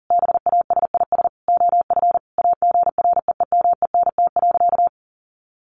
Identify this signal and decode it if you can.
Morse code, heard as a keyed tone, about 40 wpm, 700 Hz. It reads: BASIS OF AGREEMENT.
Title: Sentences from Top 1500 Words 40wpm